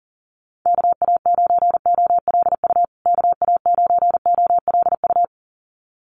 Morse code, heard as a keyed tone, about 40 wpm, 700 Hz. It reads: KA9OLU KA9OLU
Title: USA Call Signs Repeated 2x 40wpm